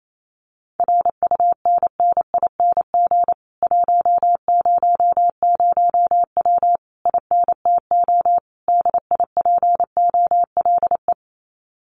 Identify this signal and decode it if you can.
Morse code, heard as a keyed tone, about 28 wpm, 700 Hz. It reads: RUNNING 100W INTO DIPOLE
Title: Common QSO Phrases Encore 28wpm